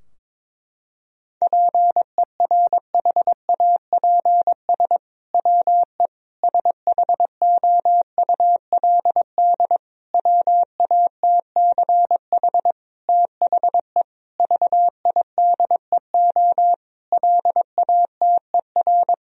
Morse code, heard as a keyed tone, about 22 wpm, 700 Hz. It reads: PERHAPS WE SHOULD WATCH THE VIDEO LATER
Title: Sentences from Top 1000 Words - Review Part 4 - 22wpm